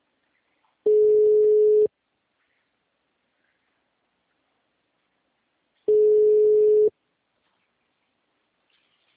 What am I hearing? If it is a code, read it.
Ringback tone (Europe, 425 Hz).
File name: out- -2016- - - .22170